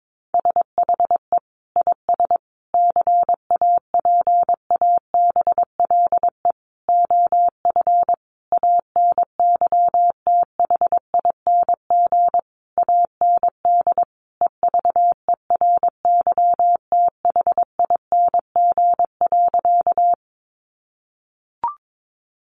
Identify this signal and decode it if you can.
Morse code, heard as a keyed tone, about 22 wpm, 700 Hz. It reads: SHE IS CAPABLE OF ANYTHING AND EVERYTHING.
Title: Sentences from Top 1200 Words 22wpm